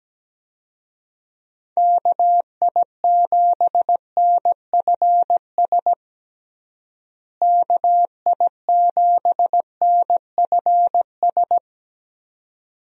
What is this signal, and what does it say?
Morse code, heard as a keyed tone, about 17 wpm, 700 Hz. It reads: KI7NFS KI7NFS
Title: USA Call Signs - 6 Characters - Repeated 2x - 17wpm